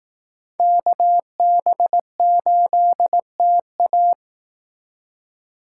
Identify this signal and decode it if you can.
Morse code, heard as a keyed tone, about 18 wpm, 700 Hz. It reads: KB8TA
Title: Speed Racing - USA 2x2 Call Signs 15wpm